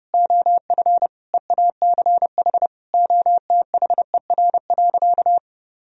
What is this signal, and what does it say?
Morse code, heard as a keyed tone, about 32 wpm, 700 Hz. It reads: OF EACH OTHER.
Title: Sentences from Top 1000 Words - Review Part 3 - 30wpm